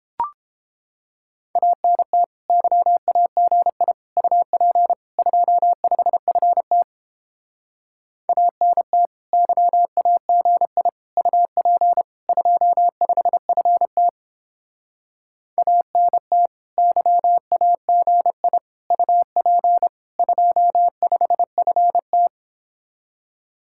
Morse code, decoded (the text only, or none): ANT YAGI UP 25FT ANT YAGI UP 25FT ANT YAGI UP 25FT